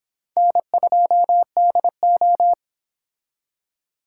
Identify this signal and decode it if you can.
Morse code, heard as a keyed tone, about 29 wpm, 700 Hz. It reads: N2DO